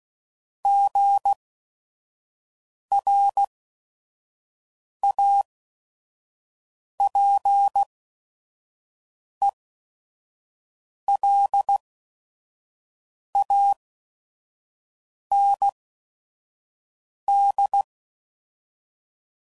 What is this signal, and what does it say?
Morse code, heard as a keyed tone, about 16 wpm, spaced as for 5 wpm, 800 Hz. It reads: GRAPELAND
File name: CW Code Groups @5WPM / fromSM's / Cities